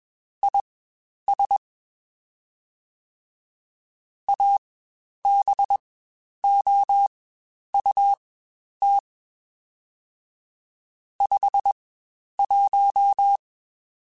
Morse code, heard as a keyed tone, about 21 wpm, spaced as for 10 wpm, 800 Hz. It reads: IS ABOUT 51
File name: Spaced Characters / Spaced Characters-mp3s / Weather 4 21 sp @20